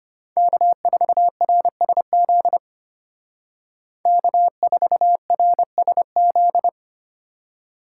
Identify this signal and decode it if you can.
Morse code, heard as a keyed tone, about 31 wpm, 700 Hz. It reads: K4RSZ K4RSZ